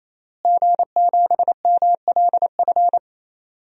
Morse code, heard as a keyed tone, about 28 wpm, 700 Hz. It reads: G7MLF
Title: DXCC-100 Call Signs - 5 Characters - Repeated 3x - 28wpm